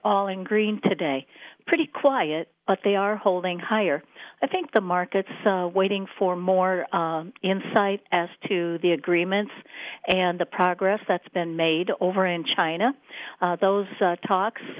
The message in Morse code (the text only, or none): none